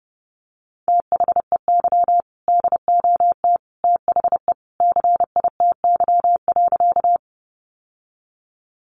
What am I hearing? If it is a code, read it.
Morse code, heard as a keyed tone, about 30 wpm, 700 Hz. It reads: THEY DOT THE CITY.
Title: Sentences from Top 1500 Words 30wpm